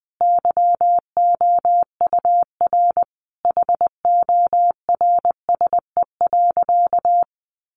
Morse code, heard as a keyed tone, about 20 wpm, 700 Hz. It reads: YOUR HORSE.
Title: Sentences from Top 1500 Words 20wpm